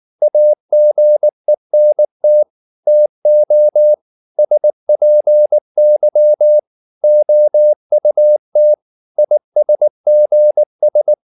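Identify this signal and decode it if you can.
Morse code, heard as a keyed tone, about 19 wpm, 600 Hz. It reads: AGENT TO SPY OUT ISGS
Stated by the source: Morse code at 19 wpm from CW med Gnister, reading Ken Follett's novel Never in English.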